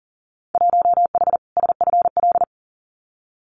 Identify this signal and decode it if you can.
Morse code, heard as a keyed tone, about 40 wpm, 700 Hz. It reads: 1H SFL
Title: Winter Field Day Exchanges Encore 40wpm